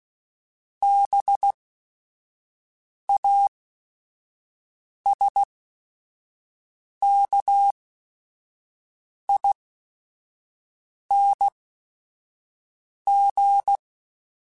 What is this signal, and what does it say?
Morse code, heard as a keyed tone, about 16 wpm, spaced as for 5 wpm, 800 Hz. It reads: BASKING